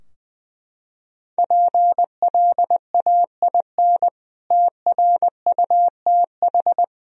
Morse code, heard as a keyed tone, about 19 wpm, 700 Hz. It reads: PLAIN TRUTH
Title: Sentences from Top 1000 Words - Review Part 2 - 20wpm